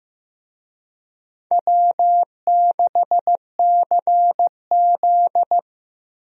Morse code, heard as a keyed tone, about 15 wpm, 700 Hz. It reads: W6CZ
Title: USA Call Signs - 4 Characters - 15wpm